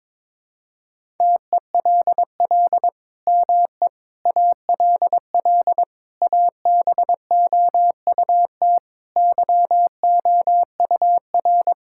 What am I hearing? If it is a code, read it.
Morse code, heard as a keyed tone, about 22 wpm, 700 Hz. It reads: TELL ME ALL ABOUT YOUR
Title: Sentences from Top 1000 Words - Review Part 6 - 22wpm